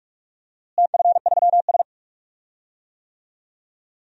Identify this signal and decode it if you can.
Morse code, heard as a keyed tone, about 41 wpm, 700 Hz. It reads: TF3S
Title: DXCC-100 Call Signs - 4 Characters - Repeated 2x - 45wpm